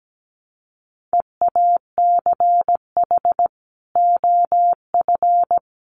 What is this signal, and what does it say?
Morse code, heard as a keyed tone, about 17 wpm, 700 Hz. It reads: EACH OF
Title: Sets of 3 Words 17wpm